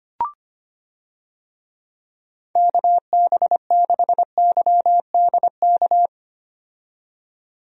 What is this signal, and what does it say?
Morse code, heard as a keyed tone, about 25 wpm, 700 Hz. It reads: KB6YDK
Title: USA Call Signs - 6 Characters - 25wpm